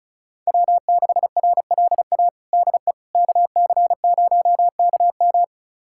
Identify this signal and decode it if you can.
Morse code, heard as a keyed tone, about 36 wpm, 700 Hz. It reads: W6RLA DE KC0KM
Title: USA Callsign DE Callsign - 5 Characters - 35wpm